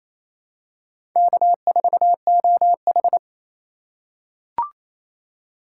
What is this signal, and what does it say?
Morse code, heard as a keyed tone, about 28 wpm, 700 Hz. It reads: K4OH
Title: USA Call Signs - 4 Characters - 4x Character Spacing - 28wpm